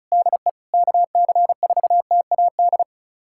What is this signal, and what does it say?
Morse code, heard as a keyed tone, about 34 wpm, 700 Hz. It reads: DE KC4TAD
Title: USA Callsign DE Callsign - 6 Characters - 35wpm